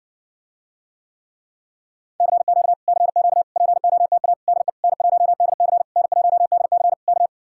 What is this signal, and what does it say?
Morse code, heard as a keyed tone, about 59 wpm, 700 Hz. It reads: CQ CQ POTA DE N1DC N1DC K